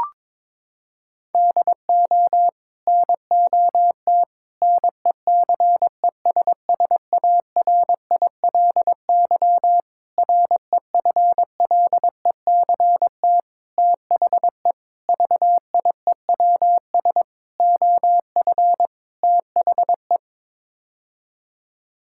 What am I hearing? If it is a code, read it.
Morse code, heard as a keyed tone, about 22 wpm, 700 Hz. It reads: DO NOT NECESSARILY REFLECT THE VIEWS OF THE
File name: practice / 22wpm / Sets of 8 Words - 22wpm